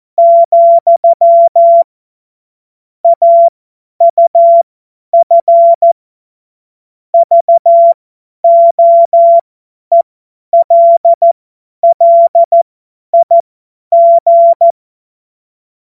Morse code, heard as a keyed tone, about 14 wpm, spaced as for 10 wpm, 680 Hz. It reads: , AUF VOELLIG